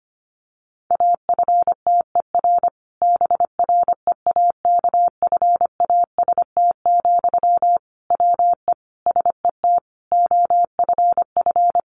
Morse code, heard as a keyed tone, about 25 wpm, 700 Hz. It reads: AFTER BREAKFAST, WE SET OFF
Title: Sentences from Top 1500 Words 25wpm